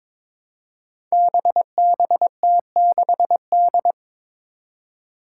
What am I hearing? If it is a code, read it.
Morse code, heard as a keyed tone, about 22 wpm, 700 Hz. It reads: BBT6D
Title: Five Challenging Letter-Number Combinations 22wpm